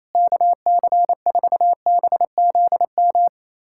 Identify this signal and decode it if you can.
Morse code, heard as a keyed tone, about 28 wpm, 700 Hz. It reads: KC4BZM
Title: USA Callsign DE Callsign - 6 Characters - 28wpm